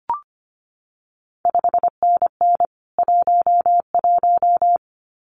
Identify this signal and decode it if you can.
Morse code, heard as a keyed tone, about 25 wpm, 700 Hz. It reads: 5NN 11
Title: CQ WW DX Contest Exchange - 25wpm